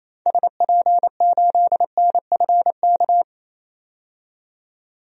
Morse code, heard as a keyed tone, about 28 wpm, 700 Hz. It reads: SP8NFK
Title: DXCC-100 Call Signs - 6 Characters - 28wpm